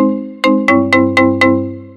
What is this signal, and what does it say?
There is no code.